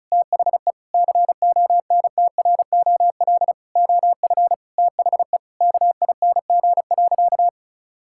Morse code, heard as a keyed tone, about 34 wpm, 700 Hz. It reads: THE CONTROL OF THE KING.
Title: Sentences from Top 1000 Words - Review Part 5 - 35wpm